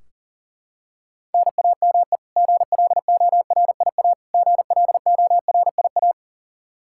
Morse code, heard as a keyed tone, about 40 wpm, 700 Hz. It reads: NAME GLORIA GLORIA
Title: K1USN SST QSO Practice 40wpm